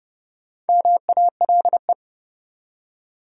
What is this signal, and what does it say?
Morse code, heard as a keyed tone, about 30 wpm, 700 Hz. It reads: MALE
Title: Four Letter Words - 30wpm